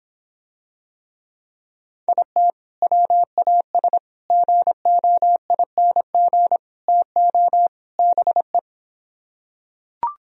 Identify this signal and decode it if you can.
Morse code, heard as a keyed tone, about 26 wpm, 700 Hz. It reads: IT WAS GOING TO BE